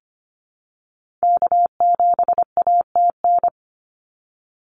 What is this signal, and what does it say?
Morse code, heard as a keyed tone, about 25 wpm, 700 Hz. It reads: K7ATN
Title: NA SOTA Activator Callsigns 25wpm